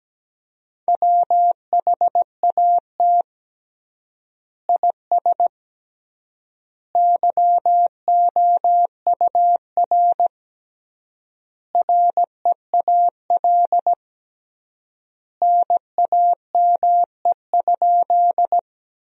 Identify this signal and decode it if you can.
Morse code, heard as a keyed tone, about 17 wpm, 700 Hz. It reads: WHAT IS YOUR REAL NAME?
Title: 3x Word-Spacing - Sentences from Top 300 Words - 17wpm